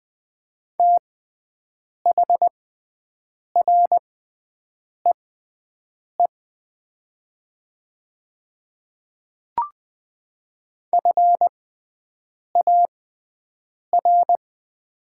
Morse code, heard as a keyed tone, about 20 wpm, spaced as for 7 wpm, 700 Hz. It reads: THREE FAR
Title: 6x Character Spacing - Top 300 Words - 20wpm